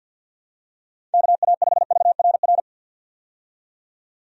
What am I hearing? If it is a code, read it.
Morse code, heard as a keyed tone, about 51 wpm, 700 Hz. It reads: KA5VRR
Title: USA Call Signs - 6 Characters - 50wpm